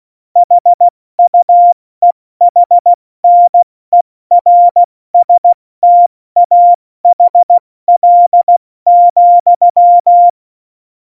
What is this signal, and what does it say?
Morse code, heard as a keyed tone, about 16 wpm, 700 Hz. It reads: HUEHNERSTAHL,
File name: Morsekurs / Witze / Witz 01. August 17, Tempo 70 F80